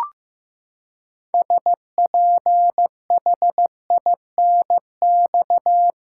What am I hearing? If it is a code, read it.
Morse code, heard as a keyed tone, about 15 wpm, 700 Hz. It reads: SPHINX
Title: Pangram Challenge v2 - 15wpm